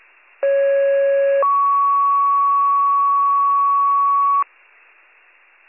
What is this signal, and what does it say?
Two-tone page, heard as a one-second tone, then a three-second tone, about 550 then 1120 Hz. It is heard from 0.4 to 4.4 s.